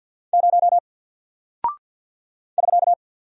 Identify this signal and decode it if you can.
Morse code, heard as a keyed tone, about 48 wpm, 700 Hz. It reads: 0 <SK>
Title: Single Character (Letter:Number:Punctuation:Prosign) Alt 50wpm